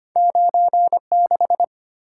Morse code, heard as a keyed tone, about 25 wpm, 700 Hz. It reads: 96
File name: Two Letter-Number Combinations - 25wpm